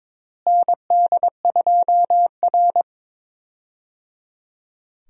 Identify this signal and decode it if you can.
Morse code, heard as a keyed tone, about 22 wpm, 700 Hz. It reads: ND2R
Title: USA Call Signs - 4 Characters - Repeated 3x - 22wpm